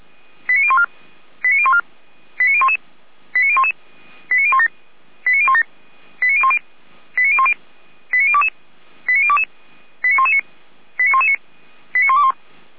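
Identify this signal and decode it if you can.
Five-tone selective call (ZVEI), heard as quick runs of five tones, tones of about 70 ms. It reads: 89014 89014 89011 89011 89017 89017 89010 89010 89022 89022 89119 89119 89121
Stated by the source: ZVEI five-tone call 89060 received on SDR.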